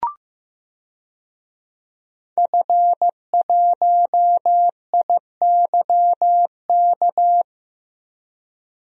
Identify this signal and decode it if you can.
Morse code, heard as a keyed tone, about 15 wpm, 700 Hz. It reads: F1IYK
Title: DXCC-100 Call Signs - 5 Characters - 15wpm